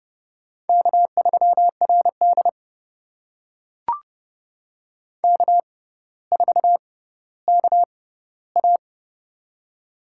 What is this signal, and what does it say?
Morse code, heard as a keyed tone, about 30 wpm, spaced as for 15 wpm, 700 Hz. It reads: K3RD K4KA
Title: USA Call Signs - 4 Characters - 6x Character Spacing - 30wpm